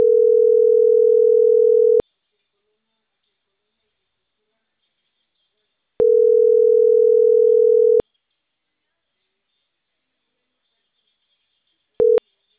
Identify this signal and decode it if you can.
Ringback tone (North America), 440 + 480 Hz.